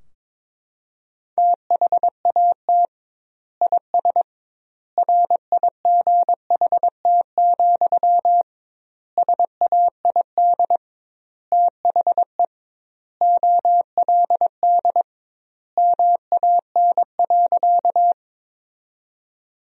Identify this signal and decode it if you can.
Morse code, heard as a keyed tone, about 22 wpm, 700 Hz. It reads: THAT IS RIGHT, SAID THE OLD MAN.
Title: 2x Word-Spacing - Sentences from Top 200 Words - 22wpm